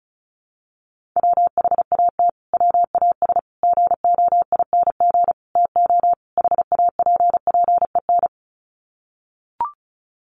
Morse code, heard as a keyed tone, about 34 wpm, 700 Hz. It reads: WHAT WAS GOING TO HAPPEN
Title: Sets of 5 Words Encore 35wpm